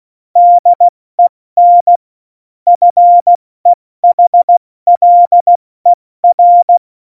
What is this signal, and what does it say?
Morse code, heard as a keyed tone, about 16 wpm, 700 Hz. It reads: DEN FEHLER
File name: Morsekurs / Witze / Witz 29. Januar 2019, 70 F80